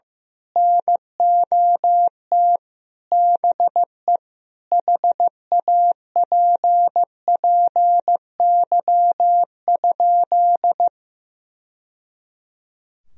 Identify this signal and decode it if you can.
Morse code, heard as a keyed tone, about 15 wpm, 700 Hz. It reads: NOT BE HAPPY?
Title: Sentences from Top 1000 Words - Review Part 2 - 15wpm